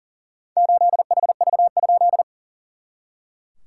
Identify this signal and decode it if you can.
Morse code, heard as a keyed tone, about 38 wpm, 690 Hz. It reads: 8HV?
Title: SOTA QSO Practice 40wpm